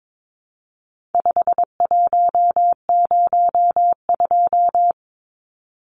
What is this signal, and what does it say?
Morse code, heard as a keyed tone, about 22 wpm, 700 Hz. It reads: 5102
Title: Numbers - 4-Digit 22wpm